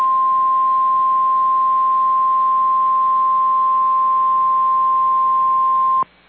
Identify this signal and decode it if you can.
Emergency Alert System (SAME) tones (the 1050 Hz attention tone). It is heard from 0.9 to 5.2 s.